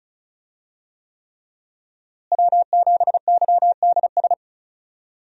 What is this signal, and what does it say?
Morse code, heard as a keyed tone, about 36 wpm, 700 Hz. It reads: W7YDS
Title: USA Call Signs - 5 Characters - Repeated 3x - 35wpm